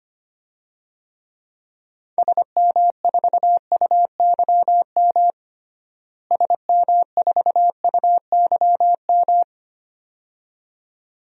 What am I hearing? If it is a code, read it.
Morse code, heard as a keyed tone, about 25 wpm, 700 Hz. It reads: SM4UYM SM4UYM
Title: DXCC-100 Call Signs - 6 Characters - Repeated 2x - 25wpm